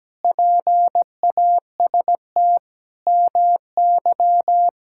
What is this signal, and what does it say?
Morse code, heard as a keyed tone, about 17 wpm, 700 Hz. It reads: PAST MY
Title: Pangram Challenge v2 - 17wpm